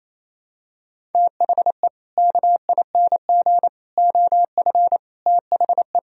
Morse code, heard as a keyed tone, about 28 wpm, 700 Hz. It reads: THE KING OF THE
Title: Sentences from Top 1000 Words - Review Part 1 - 28wpm